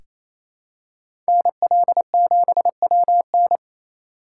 Morse code, heard as a keyed tone, about 28 wpm, 700 Hz. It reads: NL7WN
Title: USA Call Signs - 5 Characters - 2x Character Spacing - 28wpm